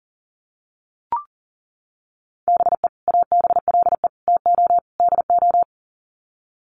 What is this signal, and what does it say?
Morse code, heard as a keyed tone, about 40 wpm, 700 Hz. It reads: BE ABLE TO DO
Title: Sets of 4 Words 40wpm